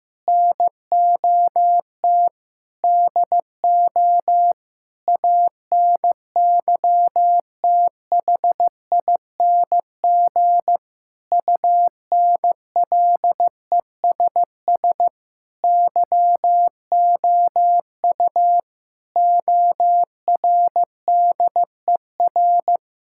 Morse code, heard as a keyed tone, about 16 wpm, 700 Hz. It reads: NOT DO ANYTHING UNLESS YOU ORDER
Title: Sentences from Top 1000 Words - Review Part 6 - 15wpm